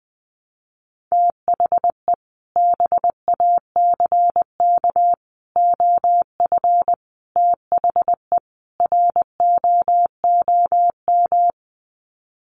Morse code, heard as a keyed tone, about 20 wpm, 700 Hz. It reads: THE BACK OF THE ROOM
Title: Sets of 5 Words Encore 20wpm